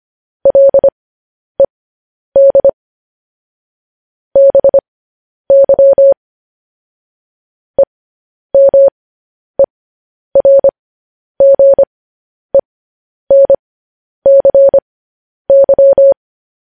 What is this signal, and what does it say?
Morse code, heard as a keyed tone, about 25 wpm, spaced as for 10 wpm, 550 Hz. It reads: LED BY EMERGENCY
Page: news headlines in morse code at 10 wpm